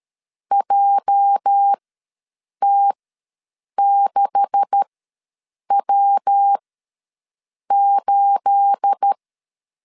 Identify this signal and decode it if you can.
Morse code, heard as a keyed tone, about 13 wpm, spaced as for 7 wpm, 800 Hz. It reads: JT6W8
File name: LEZIONI / 9 286 ws 12wpm 3351